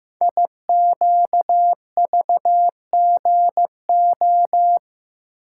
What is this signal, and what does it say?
Morse code, heard as a keyed tone, about 15 wpm, 700 Hz. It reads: IQVGO
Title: Code Groups - 5 Letters - 15wpm